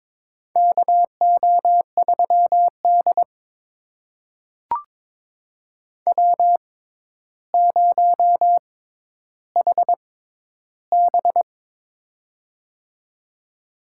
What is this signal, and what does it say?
Morse code, heard as a keyed tone, about 22 wpm, spaced as for 11 wpm, 700 Hz. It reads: KO3D W0HB
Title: USA Call Signs - 4 Characters - 6x Character Spacing - 22wpm